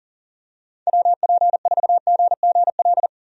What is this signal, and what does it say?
Morse code, heard as a keyed tone, about 40 wpm, 700 Hz. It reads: WP4GGL
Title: USA Call Signs - 6 Characters - Repeated 3x - 40wpm